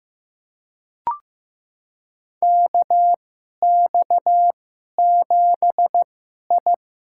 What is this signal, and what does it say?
Morse code, heard as a keyed tone, about 15 wpm, spaced as for 11 wpm, 700 Hz. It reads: KX7I
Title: USA Call Signs - 4 Characters - 2x Character Spacing - 15wpm